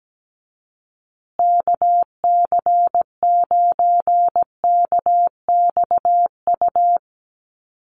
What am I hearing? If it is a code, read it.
Morse code, heard as a keyed tone, about 17 wpm, 700 Hz. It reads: KC9KXU
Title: USA Call Signs Repeated 3x 17wpm